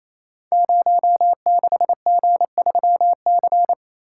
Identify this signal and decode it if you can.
Morse code, heard as a keyed tone, about 28 wpm, 700 Hz. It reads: 06G3C
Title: Code Groups - 5 Letter:Numbers - 28wpm